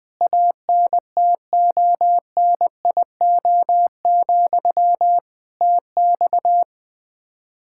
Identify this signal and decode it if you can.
Morse code, heard as a keyed tone, about 20 wpm, 700 Hz. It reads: ANTONIO, TX
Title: K1USN SST QSO Practice 20wpm